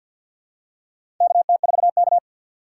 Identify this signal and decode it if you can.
Morse code, heard as a keyed tone, about 53 wpm, 700 Hz. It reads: KT4K
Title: USA Call Signs - 4 Characters - 3x Character Spacing - 50wpm